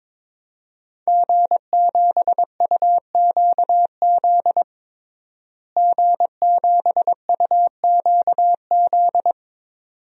Morse code, heard as a keyed tone, about 22 wpm, 700 Hz. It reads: G7UQZ G7UQZ
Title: DXCC-100 Call Signs - 5 Characters - Repeated 2x - 22wpm